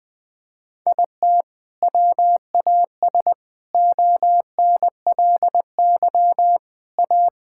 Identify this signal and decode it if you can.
Morse code, heard as a keyed tone, about 20 wpm, 700 Hz. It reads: IT WAS ONLY A
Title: Sets of 9 Words - 20wpm